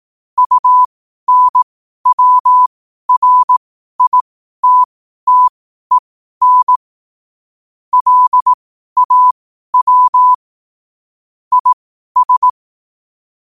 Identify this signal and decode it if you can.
Morse code, heard as a keyed tone, about 18 wpm, spaced as for 12 wpm, 1000 Hz. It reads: UNWRITTEN LAW IS
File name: qotdfiles / 12wpm